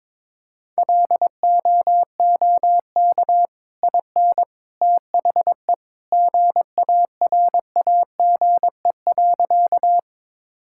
Morse code, heard as a keyed tone, about 22 wpm, 700 Hz. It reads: LOOK IN THE GARAGE.
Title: Sentences from Top 1600 Words 22wpm